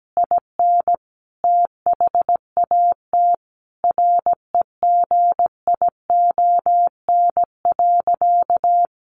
Morse code, heard as a keyed tone, about 17 wpm, 700 Hz. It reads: IN THAT REGION.
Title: Sentences from Top 1500 Words 17wpm